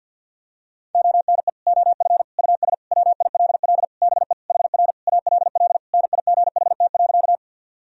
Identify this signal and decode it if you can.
Morse code, heard as a keyed tone, about 52 wpm, 700 Hz. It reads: ONE OF US WILL BE HR ALL NIGHT.
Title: Sentences from Top 300 Words and Abbreviations 50wpm